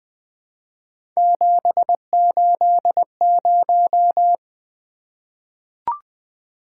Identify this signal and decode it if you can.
Morse code, heard as a keyed tone, about 20 wpm, 700 Hz. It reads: 780
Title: Three Letter-Number Combinations - 20wpm